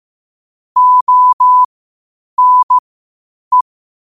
Morse code, heard as a keyed tone, about 15 wpm, spaced as for 8 wpm, 1000 Hz. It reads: ONE
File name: qotdfiles / 8wpm